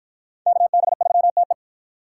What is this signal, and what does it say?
Morse code, heard as a keyed tone, about 51 wpm, 700 Hz. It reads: KB3TE